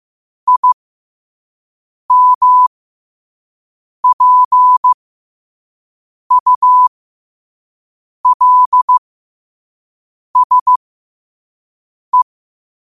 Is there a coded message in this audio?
Yes: Morse code from 0.5 to 12.2 s, heard as a keyed tone, about 15 wpm, spaced as for 5 wpm, 1000 Hz. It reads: IMPULSE